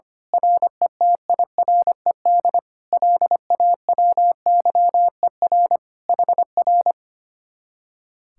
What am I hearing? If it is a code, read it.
Morse code, heard as a keyed tone, about 24 wpm, 700 Hz. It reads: RETIRED LAWYER HR
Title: Speed-Racing Common QSO Phrases - 25wpm